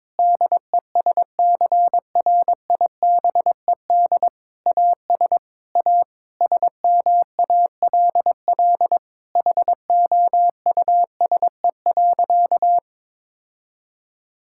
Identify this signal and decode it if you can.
Morse code, heard as a keyed tone, about 21 wpm, 700 Hz. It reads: DESCRIBED AS A SMALL HOUSE.
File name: practice / 22wpm / Sentences with 8-Word Phrases - 22wpm